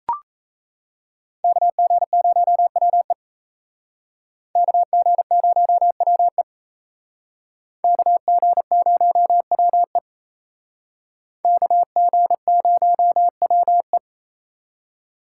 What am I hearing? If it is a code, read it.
Morse code, heard as a keyed tone, about 35 wpm, 700 Hz. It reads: KG0WE KG0WE KG0WE KG0WE
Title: Speed Racing - USA 2x2 Call Signs 28wpm